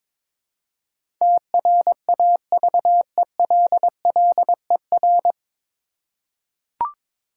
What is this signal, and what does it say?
Morse code, heard as a keyed tone, about 22 wpm, 700 Hz. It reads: TRAVELLER
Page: Morse Code Ninja practice 22wpm